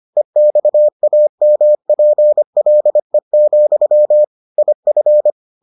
Morse code, heard as a keyed tone, about 25 wpm, 600 Hz. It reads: EXAMPLE, IF